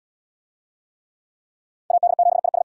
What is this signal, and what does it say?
Morse code, heard as a keyed tone, about 66 wpm, 700 Hz. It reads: AD7IS